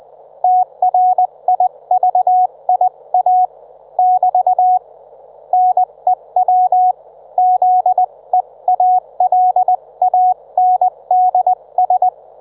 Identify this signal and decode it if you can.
Morse code, heard as a keyed tone, about 20 wpm, 730 Hz. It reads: TRIVIA <BT> NEW ZEALANDS
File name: Trivia